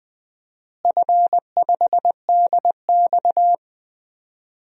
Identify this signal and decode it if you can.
Morse code, heard as a keyed tone, about 20 wpm, 700 Hz. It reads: F5DX
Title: DXCC-100 Call Signs - 4 Characters - Repeated 3x - 20wpm